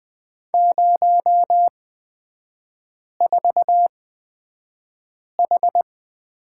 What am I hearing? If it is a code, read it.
Morse code, heard as a keyed tone, about 20 wpm, spaced as for 5 wpm, 700 Hz. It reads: 04H
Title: 250 Lesson 40e - BK - Three Characters 20wpm